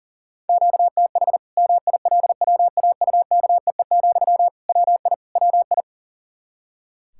Morse code, heard as a keyed tone, about 37 wpm, 700 Hz. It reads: QTH MILWAUKEE, WI WI
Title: K1USN SST QSO Practice 40wpm